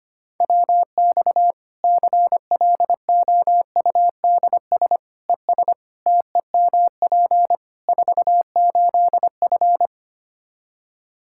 Morse code, heard as a keyed tone, about 25 wpm, 700 Hz. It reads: WX CLOUDS ES TEMP 48F
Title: Practice QSO - Exchange 2 - Info - 25wpm